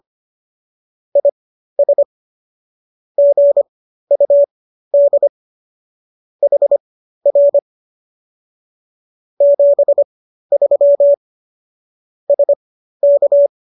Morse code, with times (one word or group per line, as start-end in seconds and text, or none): 1.2-2.0 IS
3.2-5.3 GUD
6.4-7.6 HR
9.4-11.1 73
12.3-13.5 SK